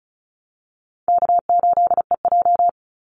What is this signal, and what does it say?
Morse code, heard as a keyed tone, about 34 wpm, 700 Hz. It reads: K8EJ
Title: Call Signs Repeated 2x - 35wpm